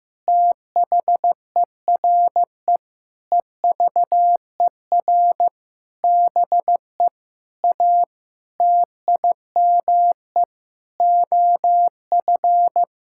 Morse code, heard as a keyed tone, about 15 wpm, 700 Hz. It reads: THERE EVER BE A TIME OF
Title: Sentences from Top 1000 Words - Review Part 5 - 15wpm